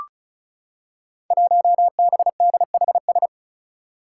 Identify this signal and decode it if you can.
Morse code, heard as a keyed tone, about 36 wpm, 700 Hz. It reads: 1BDHS
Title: Five Challenging Letter-Number Combinations 35wpm